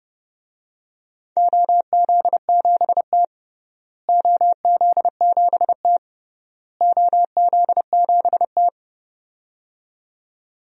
Morse code, heard as a keyed tone, about 30 wpm, 700 Hz. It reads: OZ7T OZ7T OZ7T
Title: DXCC-100 Call Signs - 4 Characters - Repeated 3x - 30wpm